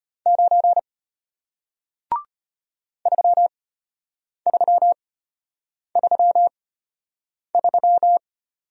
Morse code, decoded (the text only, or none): 9 3333